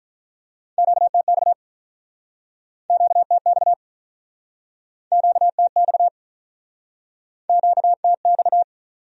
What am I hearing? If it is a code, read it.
Morse code, heard as a keyed tone, about 48 wpm, 700 Hz. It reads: QTX QTX QTX QTX